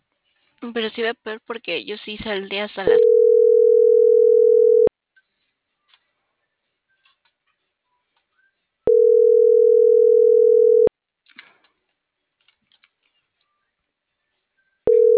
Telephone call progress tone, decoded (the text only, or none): ringback tone